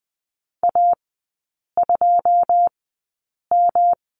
Morse code, heard as a keyed tone, about 20 wpm, spaced as for 8 wpm, 700 Hz. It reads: A2M
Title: 133 Lesson 22e - Question Mark - Three Characters 20wpm